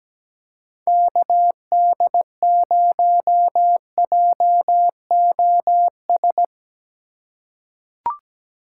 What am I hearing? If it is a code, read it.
Morse code, heard as a keyed tone, about 17 wpm, 700 Hz. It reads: KD0JOS
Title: USA Call Signs - 6 Characters - 3x Character Spacing - 17wpm